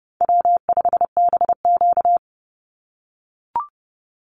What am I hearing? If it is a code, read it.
Morse code, heard as a keyed tone, about 29 wpm, 700 Hz. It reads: W5BQ
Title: Easy Call Signs - 30wpm